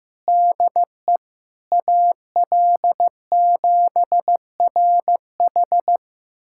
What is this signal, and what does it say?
Morse code, heard as a keyed tone, about 16 wpm, 700 Hz. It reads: DE AL7RH
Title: USA Callsign DE Callsign - 5 Characters - 15wpm